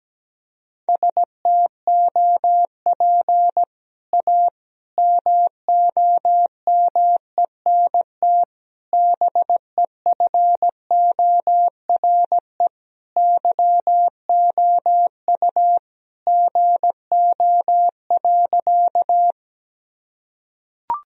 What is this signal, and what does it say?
Morse code, heard as a keyed tone, about 17 wpm, 700 Hz. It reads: STOP A MOMENT BEFORE YOU GO.
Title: Sentences from Top 1000 Words - Review Part 1 - 17wpm